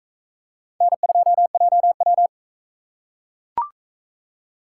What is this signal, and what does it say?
Morse code, heard as a keyed tone, about 41 wpm, 700 Hz. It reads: N2JW